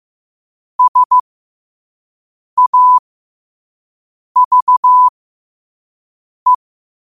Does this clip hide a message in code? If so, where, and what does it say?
Yes: Morse code from 0.8 to 6.5 s, heard as a keyed tone, about 15 wpm, spaced as for 5 wpm, 1000 Hz. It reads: SAVE